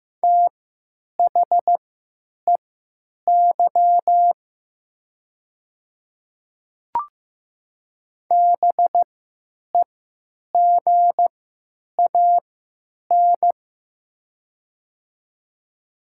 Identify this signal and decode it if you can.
Morse code, heard as a keyed tone, about 15 wpm, spaced as for 9 wpm, 700 Hz. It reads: THEY BEGAN
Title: 3x Character Spacing - Top 300 Words - 15wpm